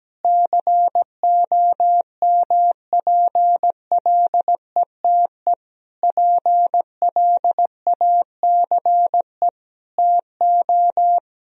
Morse code, heard as a keyed tone, about 17 wpm, 700 Hz. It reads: COMPLETE PLACE TO
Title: Sentences from Top 1000 Words - Review Part 2 - 17wpm